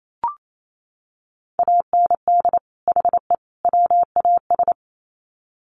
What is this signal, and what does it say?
Morse code, heard as a keyed tone, about 28 wpm, 700 Hz. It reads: AND HE WAS
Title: Sets of 3 Words 28wpm